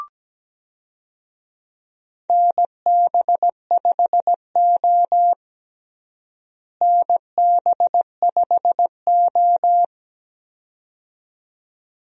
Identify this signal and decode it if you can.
Morse code, heard as a keyed tone, about 17 wpm, 700 Hz. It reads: NB5O NB5O
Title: USA Call Signs - 4 Characters - Repeated 2x - 17wpm